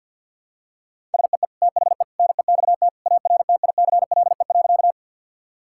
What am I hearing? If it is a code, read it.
Morse code, heard as a keyed tone, about 50 wpm, 700 Hz. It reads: SEE THE NEXT ARTICLE.
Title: Sentences from Top 1000 Words - Review Part 5 - 50wpm